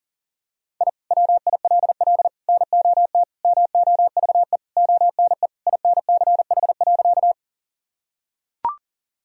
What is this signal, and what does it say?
Morse code, heard as a keyed tone, about 40 wpm, 700 Hz. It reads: I WILL NOT MOVE ONE INCH.
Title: Sentences from Top 1000 Words - Review Part 3 - 40wpm